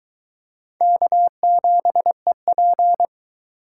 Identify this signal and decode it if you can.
Morse code, heard as a keyed tone, about 22 wpm, 700 Hz. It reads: K7EP